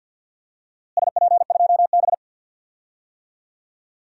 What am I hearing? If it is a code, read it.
Morse code, heard as a keyed tone, about 48 wpm, 700 Hz. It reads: SP2B
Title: DXCC-100 Call Signs - 4 Characters - Repeated 2x - 50wpm